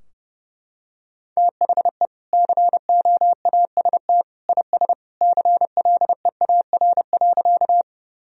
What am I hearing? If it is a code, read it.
Morse code, heard as a keyed tone, about 30 wpm, 700 Hz. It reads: THE COAST IS CLEAR.
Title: Sentences from Top 1000 Words - Review Part 4 - 30wpm